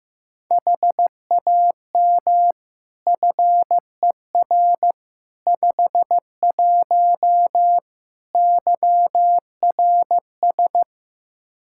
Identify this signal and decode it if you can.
Morse code, heard as a keyed tone, about 15 wpm, 700 Hz. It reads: HAM FER 51 YRS
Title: Practice QSO - Exchange 3 - More Info - 15wpm